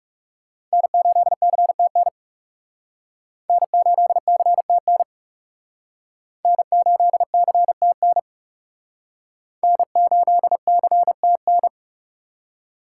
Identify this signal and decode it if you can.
Morse code, heard as a keyed tone, about 37 wpm, 700 Hz. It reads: N8CTN N8CTN N8CTN N8CTN
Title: Speed Racing - USA 1x3 Call Signs 30wpm